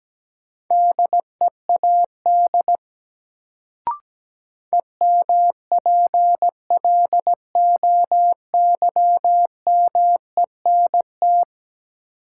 Morse code, heard as a keyed tone, about 17 wpm, 700 Hz. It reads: DEAD EMPLOYMENT
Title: Sentences from Top 1500 Words - New Word Drill - 17wpm